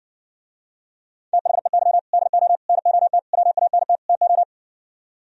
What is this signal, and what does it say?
Morse code, heard as a keyed tone, about 69 wpm, 700 Hz. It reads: THEY DO NOT WANT TO